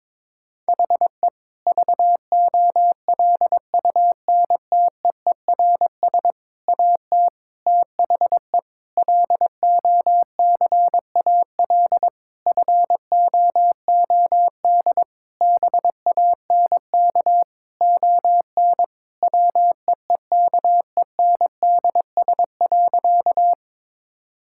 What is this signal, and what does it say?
Morse code, heard as a keyed tone, about 21 wpm, 700 Hz. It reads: HE VOLUNTEERS AT THE LOCAL FOOD BANK ON WEEKENDS.
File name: Difficult Sentences with Hints - 22wpm